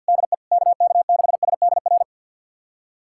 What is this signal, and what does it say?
Morse code, heard as a keyed tone, about 47 wpm, 700 Hz. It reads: DE KK6SDR
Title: USA Callsign DE Callsign - 6 Characters - 50wpm